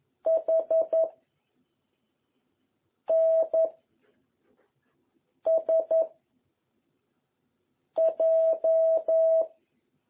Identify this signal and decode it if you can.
Morse code, heard as a keyed tone, about 11 wpm, 660 Hz. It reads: H N S J
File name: jh8cbh / 2021cw